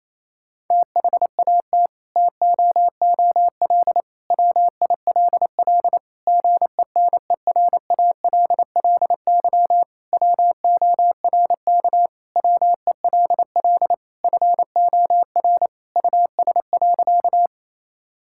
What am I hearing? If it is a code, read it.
Morse code, heard as a keyed tone, about 28 wpm, 690 Hz. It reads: THAT TOOL WILL GENERALLY WORK WELL FOR US.
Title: Sentences from Top 1000 Words - Review Part 6 - 28wpm